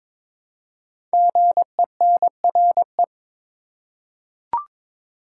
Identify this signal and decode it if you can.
Morse code, heard as a keyed tone, about 22 wpm, 700 Hz. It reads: GENRE
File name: 5 Letter Words - First 3 Letters Given - 22wpm